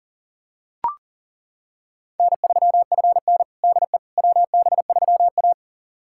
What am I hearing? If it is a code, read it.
Morse code, heard as a keyed tone, about 40 wpm, 700 Hz. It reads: N3FN DE WB3A
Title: USA Callsign DE Callsign - 4 Characters - 40wpm